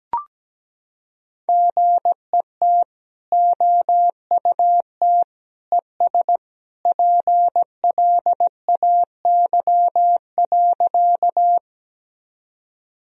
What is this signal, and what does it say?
Morse code, heard as a keyed tone, about 17 wpm, 700 Hz. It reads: GET OUT ES PLAY.
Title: Sentences from Top 200 Words and Abbreviations 17wpm